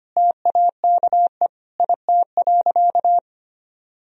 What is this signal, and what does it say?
Morse code, heard as a keyed tone, about 25 wpm, 700 Hz. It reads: TAKE IT.
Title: Sentences from Top 1000 Words - Review Part 4 - 25wpm